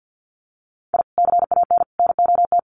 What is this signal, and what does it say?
Morse code, heard as a keyed tone, about 50 wpm, 700 Hz. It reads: I CAN NOT
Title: Sets of 5 Words Encore2 50wpm